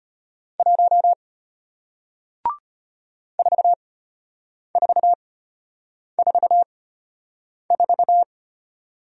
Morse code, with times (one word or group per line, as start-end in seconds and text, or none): 0.6-1.1 1
3.4-8.2 4444